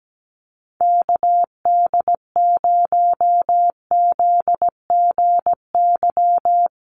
Morse code, heard as a keyed tone, about 17 wpm, 700 Hz. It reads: KD0ZGY
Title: USA Call Signs Repeated 3x 17wpm